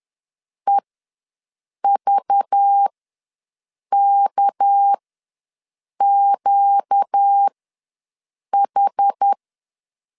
Morse code, heard as a keyed tone, about 10 wpm, spaced as for 6 wpm, 800 Hz. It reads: EVKQH
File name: LEZIONI / 7 GF9Z / 7 G ws 10wpm 3843